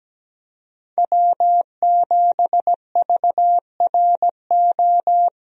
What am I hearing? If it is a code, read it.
Morse code, heard as a keyed tone, about 17 wpm, 700 Hz. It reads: W7VRO
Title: USA Call Signs - 5 Characters - Repeated 3x - 17wpm